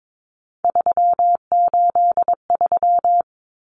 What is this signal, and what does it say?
Morse code, heard as a keyed tone, about 22 wpm, 700 Hz. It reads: 383
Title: Numbers - 3-Digit 22wpm